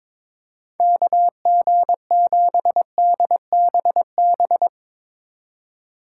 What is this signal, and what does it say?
Morse code, heard as a keyed tone, about 22 wpm, 700 Hz. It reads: KG7DBB
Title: Speed Racing - USA 2x3 Call Signs 22wpm